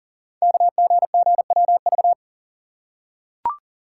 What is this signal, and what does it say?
Morse code, heard as a keyed tone, about 38 wpm, 690 Hz. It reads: KGGWV
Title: Code Groups - 5 Letters - 40wpm